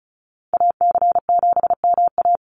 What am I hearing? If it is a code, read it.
Morse code, heard as a keyed tone, about 34 wpm, 700 Hz. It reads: AC7MA
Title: NA SOTA Activator Callsigns 35wpm